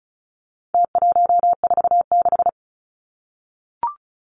Morse code, heard as a keyed tone, about 34 wpm, 700 Hz. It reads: T146
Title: Numbers - Contest Style - 4-Digit 35wpm